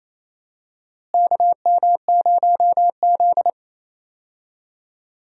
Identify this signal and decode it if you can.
Morse code, heard as a keyed tone, about 27 wpm, 700 Hz. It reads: KM0Z
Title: USA Call Signs - 4 Characters - Repeated 3x - 28wpm